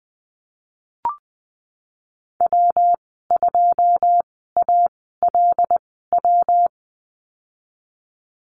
Morse code, heard as a keyed tone, about 20 wpm, spaced as for 15 wpm, 700 Hz. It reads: W2ALW
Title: 122 Lesson 20g - Number 3 - Call Signs 20wpm